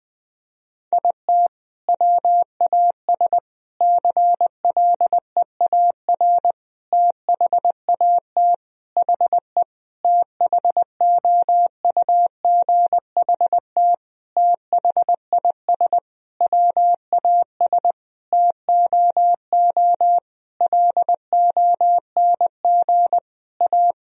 Morse code, heard as a keyed tone, about 21 wpm, 700 Hz. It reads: IT WAS CLEAR THAT HE THOUGHT THIS WAS TOO LONG A
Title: Sentences from Top 1000 Words - Review Part 4 - 20wpm